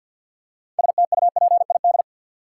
Morse code, heard as a keyed tone, about 49 wpm, 700 Hz. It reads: STUPID
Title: Six Letter Words - 50wpm